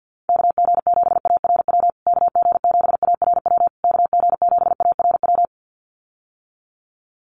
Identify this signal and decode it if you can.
Morse code, heard as a keyed tone, about 50 wpm, 700 Hz. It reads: KG7ARW KG7ARW KG7ARW
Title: USA Call Signs Repeated 3x 50wpm